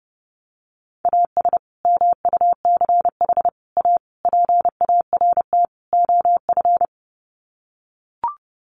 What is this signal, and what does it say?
Morse code, heard as a keyed tone, about 30 wpm, 700 Hz. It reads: AS MUCH A PART OF
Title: Sets of 5 Words Encore2 30wpm